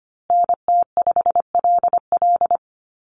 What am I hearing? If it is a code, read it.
Morse code, heard as a keyed tone, about 25 wpm, 700 Hz. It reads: NT5LL
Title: USA Call Signs Repeated 3x 25wpm